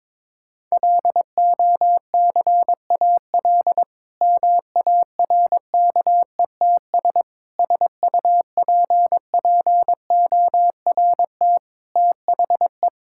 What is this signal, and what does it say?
Morse code, heard as a keyed tone, about 22 wpm, 700 Hz. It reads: LOCAL MARKETS SUPPORT THE
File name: Difficult Sentences with Hints - 22wpm